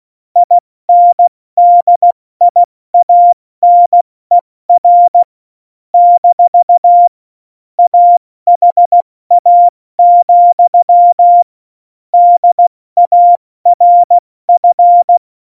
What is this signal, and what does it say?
Morse code, heard as a keyed tone, about 16 wpm, 700 Hz. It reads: INDIANER - AHA, DARF
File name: Morsekurs / Witze / Witz 06. Oktober 2020, 70 F80